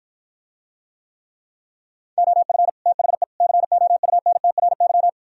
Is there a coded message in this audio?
Yes: Morse code from 2.2 to 5.1 s, heard as a keyed tone, about 52 wpm, 700 Hz. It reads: OF THE COUNTRY